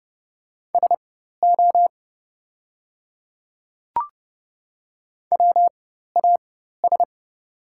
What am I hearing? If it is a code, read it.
Morse code, heard as a keyed tone, about 31 wpm, spaced as for 14 wpm, 690 Hz. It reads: SO WAS
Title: 4x Character Spacing - Top 100 Words - 30wpm